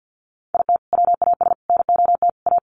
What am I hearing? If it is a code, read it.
Morse code, heard as a keyed tone, about 50 wpm, 700 Hz. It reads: IT WAS NOT A
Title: Sets of 4 Words 50wpm